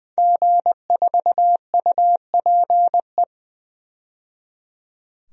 Morse code, heard as a keyed tone, about 21 wpm, 700 Hz. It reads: G4UPE
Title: DXCC-100 Call Signs - 5 Characters - 20wpm